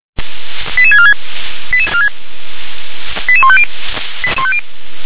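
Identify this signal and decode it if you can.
Five-tone selective call (ZVEI), heard as quick runs of five tones, tones of about 70 ms. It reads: 88646 88150
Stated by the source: ZVEI five-tone call 88150 received by SDR.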